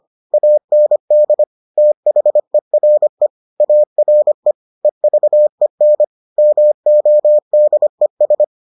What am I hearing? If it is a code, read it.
Morse code, heard as a keyed tone, about 25 wpm, 600 Hz. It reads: AND THERE ARE EVEN MODES